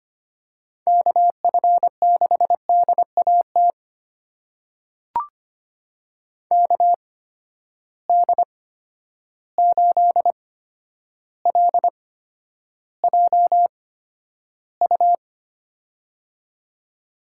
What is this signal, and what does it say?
Morse code, heard as a keyed tone, about 25 wpm, spaced as for 11 wpm, 700 Hz. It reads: KF6DAT KD8LJU
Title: USA Call Signs - 6 Characters - 8x Character Spacing - 25wpm